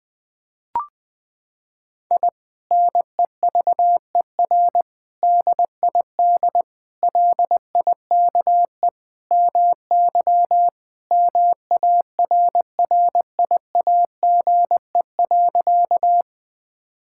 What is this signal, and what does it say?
Morse code, heard as a keyed tone, about 20 wpm, 700 Hz. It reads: I NEVER DID LIKE MY MARRIAGE.
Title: Sentences from Top 1000 Words - Review Part 5 - 20wpm